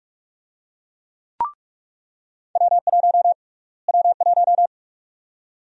Morse code, heard as a keyed tone, about 46 wpm, 700 Hz. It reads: W1 W1
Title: USA Call Sign Prefixes - Repeated 2x - 45wpm